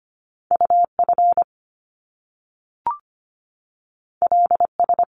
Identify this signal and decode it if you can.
Morse code, heard as a keyed tone, about 25 wpm, 700 Hz. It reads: UF LS
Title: Top Two Letter Combinations in Words 25wpm